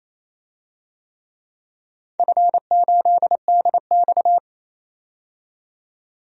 Morse code, heard as a keyed tone, about 28 wpm, 700 Hz. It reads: F8DX
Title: DXCC-100 Call Signs - 4 Characters - 28wpm